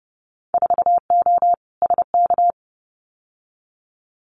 Morse code, heard as a keyed tone, about 31 wpm, 700 Hz. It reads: 4O SK
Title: Winter Field Day Exchanges Encore 30wpm